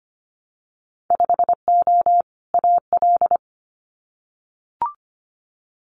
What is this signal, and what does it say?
Morse code, heard as a keyed tone, about 25 wpm, 700 Hz. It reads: 5O AL
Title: Winter Field Day Exchanges 25wpm